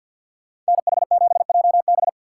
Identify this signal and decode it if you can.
Morse code, heard as a keyed tone, about 65 wpm, spaced as for 57 wpm, 700 Hz. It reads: NH7JB
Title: Speed Racing - USA 2x2 Call Signs 50wpm